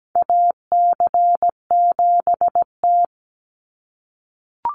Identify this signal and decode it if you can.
Morse code, heard as a keyed tone, about 17 wpm, 700 Hz. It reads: AC7T
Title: Call Signs - 17wpm